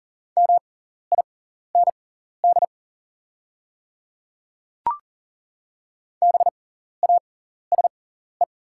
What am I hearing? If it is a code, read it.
Morse code, heard as a keyed tone, about 40 wpm, spaced as for 14 wpm, 700 Hz. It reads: MIND BASE
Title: 6x Character Spacing - Top 500 Words - 40wpm